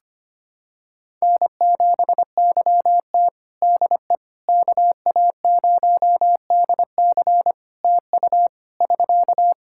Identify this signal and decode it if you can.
Morse code, heard as a keyed tone, about 25 wpm, 700 Hz. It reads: N7YT DE KA0DC TU <SK>
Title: Practice QSO - Exchange 4 - Ending - 25wpm